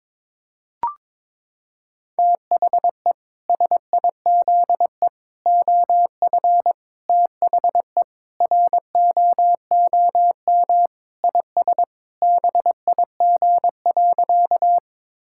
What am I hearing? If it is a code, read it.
Morse code, heard as a keyed tone, about 22 wpm, 700 Hz. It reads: THE SIZE OF THE ROOM IS BIG.
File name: practice / 22wpm / Sentences with 3-Word Phrases - 22wpm